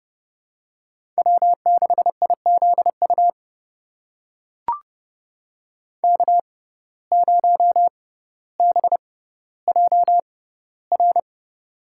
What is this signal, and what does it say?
Morse code, heard as a keyed tone, about 30 wpm, spaced as for 15 wpm, 700 Hz. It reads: W6IZU K0BJR
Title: USA Call Signs - 6x Character Spacing - 30wpm